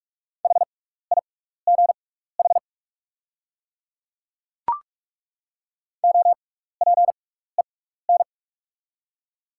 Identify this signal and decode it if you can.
Morse code, heard as a keyed tone, about 42 wpm, spaced as for 15 wpm, 690 Hz. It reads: HIGH OPEN
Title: 6x Character Spacing - Top 300 Words - 45wpm